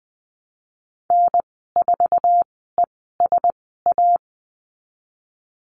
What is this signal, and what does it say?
Morse code, heard as a keyed tone, about 20 wpm, spaced as for 15 wpm, 700 Hz. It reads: N4ESA
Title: 055 Lesson 10g - Letter r - Call Signs 20wpm